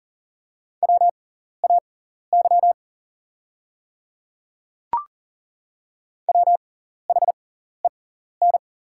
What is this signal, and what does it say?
Morse code, heard as a keyed tone, about 40 wpm, spaced as for 14 wpm, 690 Hz. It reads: WAY WHEN
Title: 6x Character Spacing - Top 100 Words - 40wpm